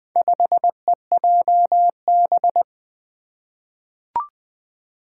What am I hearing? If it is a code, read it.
Morse code, heard as a keyed tone, about 20 wpm, 700 Hz. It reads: 5EJB
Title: Four Challenging Letter-Number Combinations 20wpm